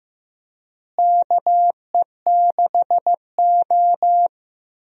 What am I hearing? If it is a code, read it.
Morse code, heard as a keyed tone, about 15 wpm, 700 Hz. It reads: KE6O
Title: USA Call Signs - 4 Characters - Repeated 3x - 15wpm